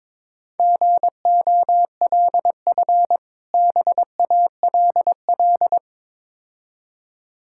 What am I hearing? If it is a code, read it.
Morse code, heard as a keyed tone, about 23 wpm, 690 Hz. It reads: GOLF BALL
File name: Difficult Words with Hints - 22wpm